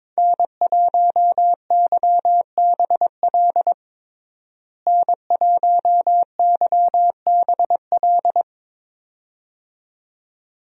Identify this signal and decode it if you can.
Morse code, heard as a keyed tone, about 22 wpm, 700 Hz. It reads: N1YBL N1YBL
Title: USA Call Signs - 5 Characters - Repeated 2x - 22wpm